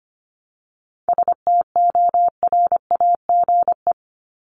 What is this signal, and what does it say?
Morse code, heard as a keyed tone, about 25 wpm, 700 Hz. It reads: STORAGE
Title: Top 1251-1500 Words 25wpm